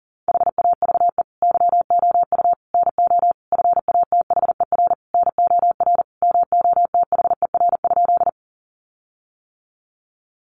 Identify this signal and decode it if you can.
Morse code, heard as a keyed tone, about 40 wpm, 700 Hz. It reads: HAVE YOU NO FATHER NOR MOTHER?
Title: Sentences from Top 1500 Words 40wpm